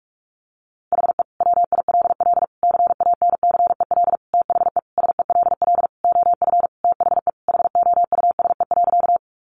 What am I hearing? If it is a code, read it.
Morse code, heard as a keyed tone, about 44 wpm, 700 Hz. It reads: HE WILL CANCEL THE SELL OF THE HOUSE.
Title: Sentences from Top 1200 Words 45wpm